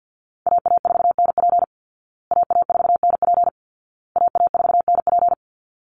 Morse code, heard as a keyed tone, about 50 wpm, 700 Hz. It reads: AA4NP AA4NP AA4NP
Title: Call Signs Repeated 2x - 50wpm